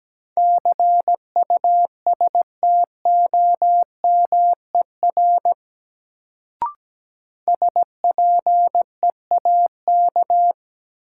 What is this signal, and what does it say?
Morse code, heard as a keyed tone, about 17 wpm, 700 Hz. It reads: CUSTOMER SPEAK
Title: Sentences from Top 1100 Words - New Word Drill Alt - 17wpm